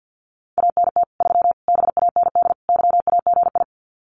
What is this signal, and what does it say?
Morse code, heard as a keyed tone, about 50 wpm, 700 Hz. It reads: ANT 3 BAND YAGI
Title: Common QSO Phrases 50wpm